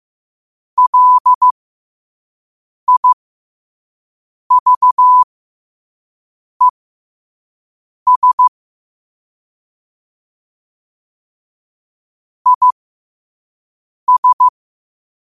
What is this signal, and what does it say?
Morse code, heard as a keyed tone, about 15 wpm, spaced as for 5 wpm, 1000 Hz. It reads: LIVES IS
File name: qotdfiles / 5wpm